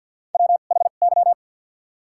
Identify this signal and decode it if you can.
Morse code, heard as a keyed tone, about 52 wpm, 700 Hz. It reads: W H Y
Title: 2x Character Spacing - Top 500 Words - 50wpm